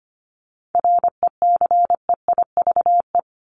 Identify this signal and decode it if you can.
Morse code, heard as a keyed tone, about 25 wpm, 700 Hz. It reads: RECEIVE
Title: Top 751-1000 Words 25wpm